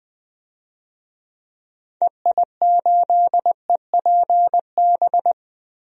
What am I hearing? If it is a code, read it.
Morse code, heard as a keyed tone, about 20 wpm, 700 Hz. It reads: EI8EPB
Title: DXCC-100 Call Signs - 6 Characters - Repeated 2x - 20wpm